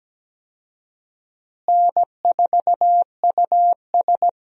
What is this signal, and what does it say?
Morse code, heard as a keyed tone, about 17 wpm, 700 Hz. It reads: N4US
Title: USA Call Signs - 4 Characters - Repeated 3x - 17wpm